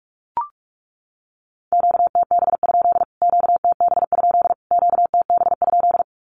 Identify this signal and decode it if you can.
Morse code, heard as a keyed tone, about 44 wpm, 700 Hz. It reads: QTB? QTB? QTB?
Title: Q-Codes Repeated 3x 45wpm